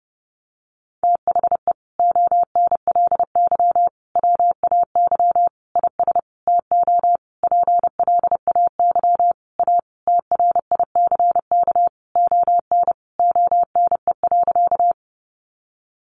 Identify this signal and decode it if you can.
Morse code, heard as a keyed tone, about 30 wpm, 700 Hz. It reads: THE ONLY WAY IS TO PLAY A TRICK ON ONE.
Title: Sentences from Top 1500 Words 30wpm